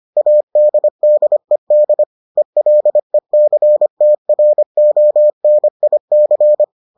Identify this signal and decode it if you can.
Morse code, heard as a keyed tone, about 25 wpm, 600 Hz. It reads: ADDED ELECTRONIC